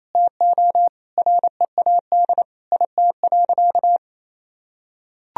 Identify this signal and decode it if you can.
Morse code, heard as a keyed tone, about 28 wpm, 700 Hz. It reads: TO READ IT.
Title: Sentences from Top 500 Words and Abbreviations 28wpm